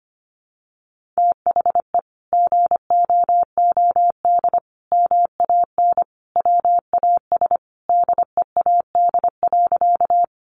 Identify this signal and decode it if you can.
Morse code, heard as a keyed tone, about 25 wpm, 700 Hz. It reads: THE GOOD MAN WAS DEAD.
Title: Sentences from Top 1500 Words 25wpm